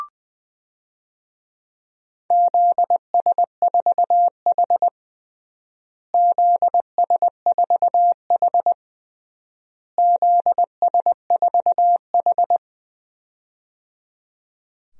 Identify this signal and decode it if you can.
Morse code, heard as a keyed tone, about 20 wpm, 700 Hz. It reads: ZS4H ZS4H ZS4H
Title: DXCC-100 Call Signs - 4 Characters - Repeated 3x - 20wpm